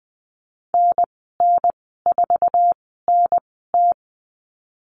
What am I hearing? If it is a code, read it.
Morse code, heard as a keyed tone, about 20 wpm, spaced as for 14 wpm, 700 Hz. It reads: NN4NT
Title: 062 Lesson 11g - Letter h - Call Signs 20wpm